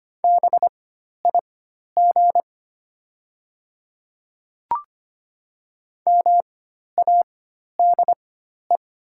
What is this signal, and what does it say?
Morse code, heard as a keyed tone, about 25 wpm, spaced as for 12 wpm, 700 Hz. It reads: BIG MADE
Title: 4x Character Spacing - Top 300 Words - 25wpm